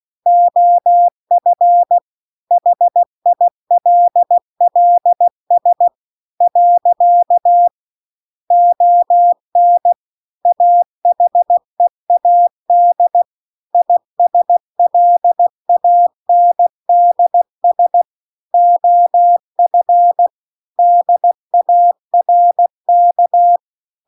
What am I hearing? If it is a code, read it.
Morse code, heard as a keyed tone, about 16 wpm, 700 Hz. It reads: OF HILLS . ON AHEAD ISLANDS OF DARK